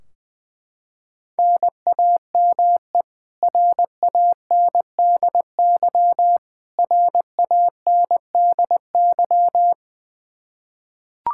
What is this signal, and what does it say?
Morse code, heard as a keyed tone, about 20 wpm, 700 Hz. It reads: NAME RANDY RANDY
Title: Practice QSO - Exchange 1 - Beginning - 20wpm